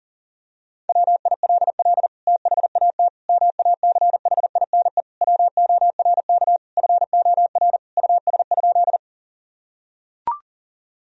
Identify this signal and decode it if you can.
Morse code, heard as a keyed tone, about 40 wpm, 700 Hz. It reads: WILL THAT MACHINE WORK FOR US?
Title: Sentences from Top 1000 Words - Review Part 2 - 40wpm